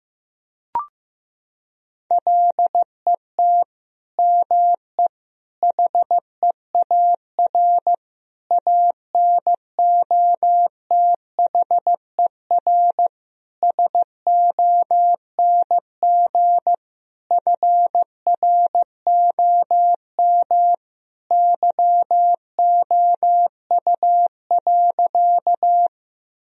Morse code, heard as a keyed tone, about 15 wpm, 700 Hz. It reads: LET ME HEAR ANOTHER SONG FROM YOU.
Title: Sentences from Top 1000 Words - Review Part 2 - 15wpm